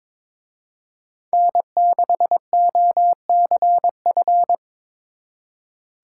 Morse code, heard as a keyed tone, about 24 wpm, 700 Hz. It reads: N6OCF
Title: Speed Racing - USA 1x3 Call Signs 22wpm